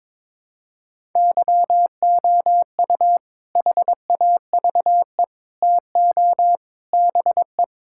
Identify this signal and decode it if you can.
Morse code, heard as a keyed tone, about 22 wpm, 700 Hz. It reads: YOU HAVE TO BE
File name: practice / 22wpm / Sentences with 4-Word Phrases - 22wpm